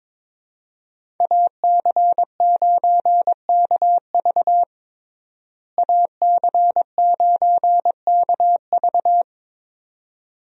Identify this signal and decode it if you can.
Morse code, heard as a keyed tone, about 22 wpm, 700 Hz. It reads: AC9KV AC9KV
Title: USA Call Signs - 5 Characters - Repeated 2x - 22wpm